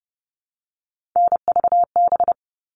Morse code, heard as a keyed tone, about 30 wpm, 700 Hz. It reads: NVB
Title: All Cut Numbers - 3-Digit 30wpm